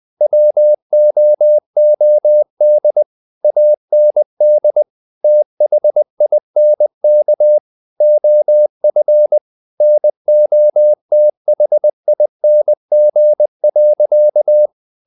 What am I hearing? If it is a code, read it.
Morse code, heard as a keyed tone, about 20 wpm, 600 Hz. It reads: WOOD AND THINK OF NOTHING.
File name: the old man and the sea / the old man and the sea part 1 20wpm